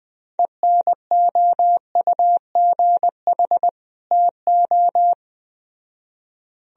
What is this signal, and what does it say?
Morse code, heard as a keyed tone, about 20 wpm, 700 Hz. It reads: ENOUGH TO